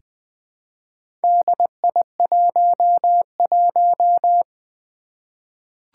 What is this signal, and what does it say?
Morse code, heard as a keyed tone, about 20 wpm, 700 Hz. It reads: DI11
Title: Four Challenging Letter-Number Combinations 20wpm